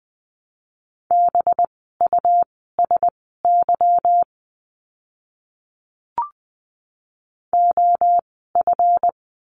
Morse code, heard as a keyed tone, about 20 wpm, spaced as for 15 wpm, 700 Hz. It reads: BUSY OF
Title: 185 Lesson 30f - Letter b - Words 20wpm